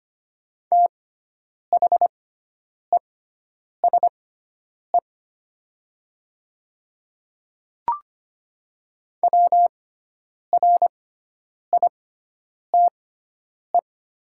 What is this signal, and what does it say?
Morse code, heard as a keyed tone, about 25 wpm, spaced as for 9 wpm, 700 Hz. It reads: THESE WRITE
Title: 6x Character Spacing - Top 200 Words - 25wpm